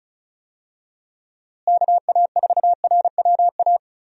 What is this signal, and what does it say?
Morse code, heard as a keyed tone, about 34 wpm, 700 Hz. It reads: KA4RWA
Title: USA Call Signs - 6 Characters - Repeated 2x - 35wpm